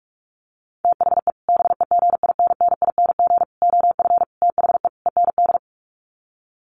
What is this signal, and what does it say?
Morse code, heard as a keyed tone, about 44 wpm, 700 Hz. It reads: THE BEGINNING OF THE END
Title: Sets of 5 Words Encore2 45wpm